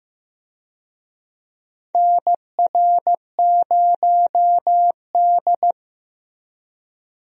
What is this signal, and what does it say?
Morse code, heard as a keyed tone, about 15 wpm, 700 Hz. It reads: NR0D
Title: USA Call Signs - 4 Characters - Repeated 3x - 15wpm